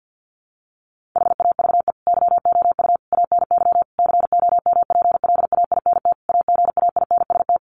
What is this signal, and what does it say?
Morse code, heard as a keyed tone, about 50 wpm, 700 Hz. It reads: HAVE YOU ANY COMPLAINT AGAINST
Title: Sentences from Top 1400 Words 50wpm